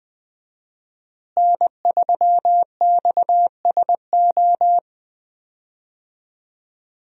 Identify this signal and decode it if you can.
Morse code, heard as a keyed tone, about 20 wpm, 700 Hz. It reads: N3XSO
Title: USA Call Signs - 5 Characters - Repeated 3x - 20wpm